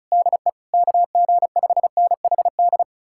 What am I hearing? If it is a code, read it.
Morse code, heard as a keyed tone, about 46 wpm, 700 Hz. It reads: DE KG5NHD